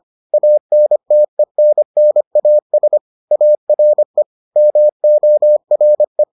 Morse code, heard as a keyed tone, about 25 wpm, 600 Hz. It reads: ANTENNAS ARE MORE